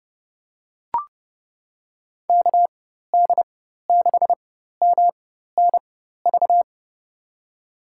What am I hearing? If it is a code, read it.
Morse code, heard as a keyed tone, about 30 wpm, spaced as for 14 wpm, 700 Hz. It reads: KD6MNV
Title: USA Call Signs - 6 Characters - 4x Character Spacing - 30wpm